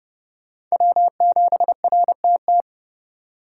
Morse code, heard as a keyed tone, about 37 wpm, 700 Hz. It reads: W7RTT
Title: Speed Racing - USA 1x3 Call Signs 30wpm